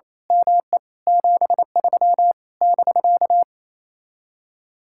Morse code, heard as a keyed tone, about 26 wpm, 700 Hz. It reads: ME 73 <BK>
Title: Speed-Racing Common QSO Phrases - 28wpm